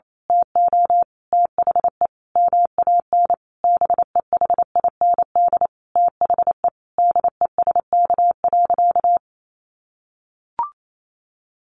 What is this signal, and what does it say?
Morse code, heard as a keyed tone, about 28 wpm, 700 Hz. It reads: TO THE MAN BEHIND THE DESK.
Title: A Sound Of Thunder 28wpm